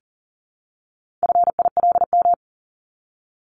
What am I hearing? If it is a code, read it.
Morse code, heard as a keyed tone, about 38 wpm, 700 Hz. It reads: FILM
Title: Top 501-750 Words 40wpm